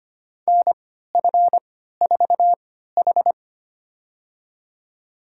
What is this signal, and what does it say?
Morse code, heard as a keyed tone, about 25 wpm, spaced as for 14 wpm, 700 Hz. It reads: NF4H
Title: USA Call Signs - 4 Characters - 3x Character Spacing - 25wpm